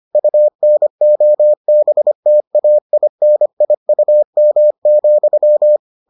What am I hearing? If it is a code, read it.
Morse code, heard as a keyed tone, about 25 wpm, 600 Hz. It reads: UNOBTAINIUM,